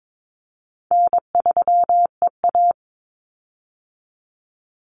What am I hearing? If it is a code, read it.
Morse code, heard as a keyed tone, about 22 wpm, 700 Hz. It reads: N3EA
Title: Easy Call Signs - 22wpm